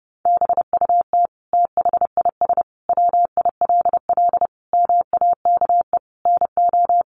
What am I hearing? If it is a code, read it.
Morse code, heard as a keyed tone, about 30 wpm, 700 Hz. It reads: BUT THIS WILL MAKE NO
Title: Sentences from Top 800 Words 30wpm